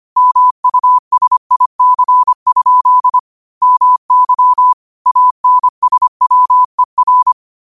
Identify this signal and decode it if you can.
Morse code, heard as a keyed tone, about 25 wpm, 1000 Hz. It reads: MUSIC? MY ANSWER
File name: qotdfiles / 25wpm